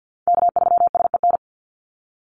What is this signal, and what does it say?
Morse code, heard as a keyed tone, about 50 wpm, 700 Hz. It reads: K3SEN
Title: Call Signs Repeated 2x - 50wpm